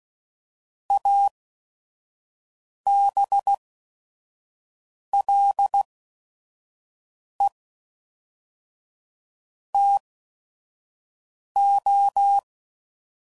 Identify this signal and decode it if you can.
Morse code, heard as a keyed tone, about 16 wpm, spaced as for 5 wpm, 800 Hz. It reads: ABLE TO
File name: CW Code Groups @5WPM / QSOs / QSO WA7QOX 1